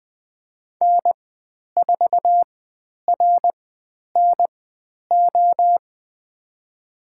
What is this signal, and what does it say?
Morse code, heard as a keyed tone, about 20 wpm, spaced as for 10 wpm, 700 Hz. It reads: N4RNO
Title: 102 Lesson 17g - Letter u - Call Signs 20wpm